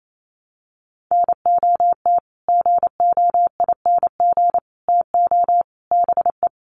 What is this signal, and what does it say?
Morse code, heard as a keyed tone, about 28 wpm, 700 Hz. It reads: NOT GOING TO BE
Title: Sets of 5 Words 28wpm